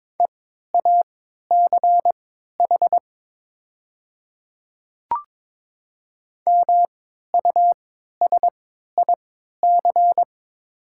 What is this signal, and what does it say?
Morse code, heard as a keyed tone, about 22 wpm, spaced as for 12 wpm, 700 Hz. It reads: EACH MUSIC
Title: 3x Character Spacing - Top 300 Words - 22wpm